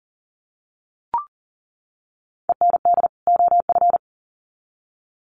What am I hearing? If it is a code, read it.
Morse code, heard as a keyed tone, about 40 wpm, 700 Hz. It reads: END OF
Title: Sets of 2 Words 40wpm